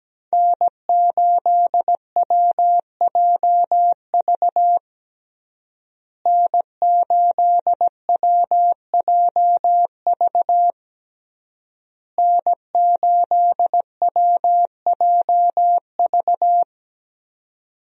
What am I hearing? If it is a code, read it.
Morse code, heard as a keyed tone, about 17 wpm, 700 Hz. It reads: N8WJV N8WJV N8WJV
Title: USA Call Signs - 5 Characters - Repeated 3x - 17wpm